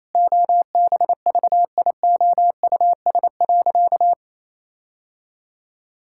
Morse code, heard as a keyed tone, about 27 wpm, 700 Hz. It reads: OBVIOUS.
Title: Sentences from Top 1600 Words 28wpm